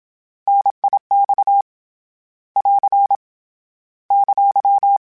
Morse code, heard as a keyed tone, about 27 wpm, 800 Hz. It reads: NIX <AR> !
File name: Contrasts / Contrasts-mp3s / 3LtrContrasts @25